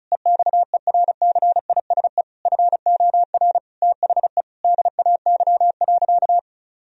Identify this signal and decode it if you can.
Morse code, heard as a keyed tone, about 36 wpm, 700 Hz. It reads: EXERCISE FOR THE DAY.
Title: Sentences from Top 1000 Words - Review Part 4 - 35wpm